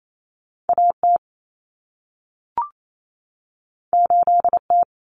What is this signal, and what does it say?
Morse code, heard as a keyed tone, about 28 wpm, 700 Hz. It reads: AT 8T
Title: Common Cut Numbers - 2-Digit 28wpm